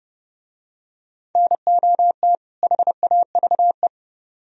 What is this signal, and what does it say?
Morse code, heard as a keyed tone, about 30 wpm, 700 Hz. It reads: NOT HAVE